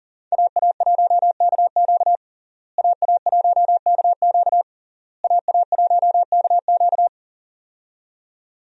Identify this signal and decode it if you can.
Morse code, heard as a keyed tone, about 40 wpm, 700 Hz. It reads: AA1KQ AA1KQ AA1KQ
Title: USA Call Signs - 5 Characters - Repeated 3x - 40wpm